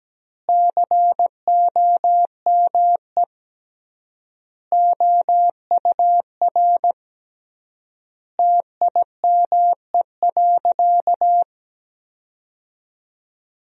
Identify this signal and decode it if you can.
Morse code, heard as a keyed tone, about 17 wpm, 700 Hz. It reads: COME OUR TIME.
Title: 3x Word-Spacing - Sentences from Top 200 Words - 17wpm